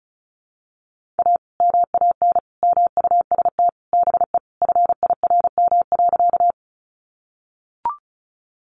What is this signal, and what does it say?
Morse code, heard as a keyed tone, about 34 wpm, 700 Hz. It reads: A MAN MUST BE FIRM.
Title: Sentences from Top 1400 Words 35wpm